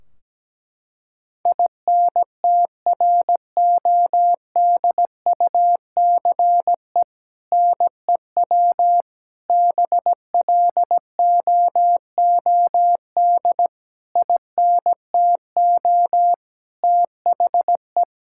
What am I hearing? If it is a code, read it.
Morse code, heard as a keyed tone, about 17 wpm, 700 Hz. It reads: INTRODUCE NEW BLOOD INTO THE
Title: Sentences from Top 1600 Words 17wpm